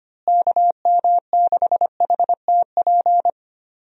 Morse code, heard as a keyed tone, about 25 wpm, 700 Hz. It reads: KM6HTP
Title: USA Call Signs - 6 Characters - Repeated 2x - 25wpm